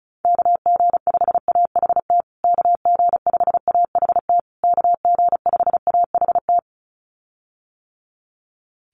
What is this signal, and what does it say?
Morse code, heard as a keyed tone, about 34 wpm, 700 Hz. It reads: KG5AHT KG5AHT KG5AHT
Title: USA Call Signs Repeated 3x 35wpm